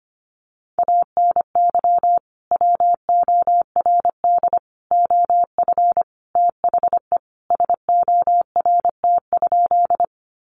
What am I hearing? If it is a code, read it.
Morse code, heard as a keyed tone, about 25 wpm, 700 Hz. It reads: ANY WORD OF THE SORT?
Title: Sentences from Top 1300 Words 25wpm